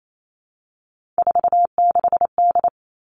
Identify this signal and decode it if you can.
Morse code, heard as a keyed tone, about 28 wpm, 700 Hz. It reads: 46D
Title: All Cut Numbers - 3-Digit 28wpm